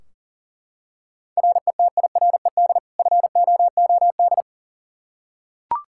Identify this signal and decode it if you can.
Morse code, heard as a keyed tone, about 40 wpm, 700 Hz. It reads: RETIRED FOOD
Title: Practice QSO - Exchange 3 - More Info - 40wpm